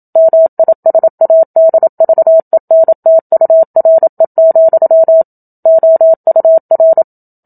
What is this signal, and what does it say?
Morse code, heard as a keyed tone, about 27 wpm, 650 Hz. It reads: MISADVENTURE, OUR
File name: Book Two / 01-Chapter 1 - 27 WPM